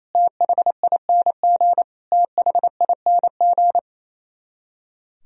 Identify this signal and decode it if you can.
Morse code, heard as a keyed tone, about 28 wpm, 700 Hz. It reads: THING THING
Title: Top 100 Words Repeated 28wpm